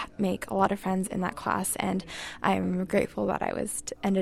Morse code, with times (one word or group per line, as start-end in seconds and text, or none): none